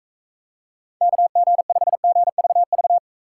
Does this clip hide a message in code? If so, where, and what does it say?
Yes: Morse code from 1.0 to 3.0 s, heard as a keyed tone, about 44 wpm, 700 Hz. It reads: KG5GVV